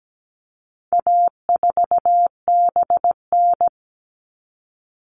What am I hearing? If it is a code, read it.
Morse code, heard as a keyed tone, about 17 wpm, 700 Hz. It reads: A4BN
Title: All Cut Numbers - 4-Digit 17wpm